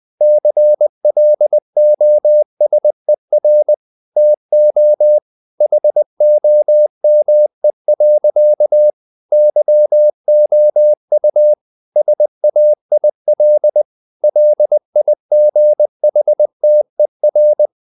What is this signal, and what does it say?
Morse code, heard as a keyed tone, about 20 wpm, 600 Hz. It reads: CLOSER TO HOME. YOU SAIL LIGHTER
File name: the old man and the sea part 2 20wpm